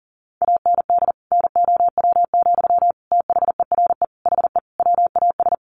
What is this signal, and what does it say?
Morse code, heard as a keyed tone, about 40 wpm, 700 Hz. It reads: AND NOW, THERE HE WAS
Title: Sentences from Top 1500 Words 40wpm